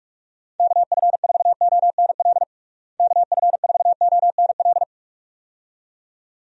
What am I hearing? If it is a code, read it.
Morse code, heard as a keyed tone, about 41 wpm, 700 Hz. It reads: KF4ONL KF4ONL
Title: USA Call Signs - 6 Characters - Repeated 2x - 45wpm